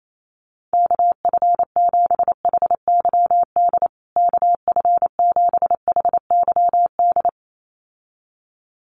Morse code, heard as a keyed tone, about 28 wpm, 700 Hz. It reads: KF7HYD KF7HYD
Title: USA Call Signs Repeated 2x 28wpm